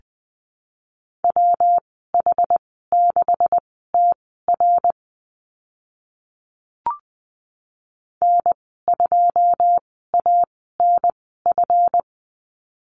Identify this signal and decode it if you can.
Morse code, heard as a keyed tone, about 19 wpm, spaced as for 14 wpm, 700 Hz. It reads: WH6TR N2ANF
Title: 140 Lesson 23g - Letter f - Call Signs 20wpm